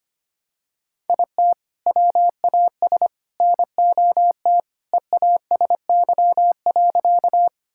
Morse code, heard as a keyed tone, about 25 wpm, 700 Hz. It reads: IT WAS NOT EASY.
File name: practice / 25wpm / Sentences with 3-Word Phrases - 25wpm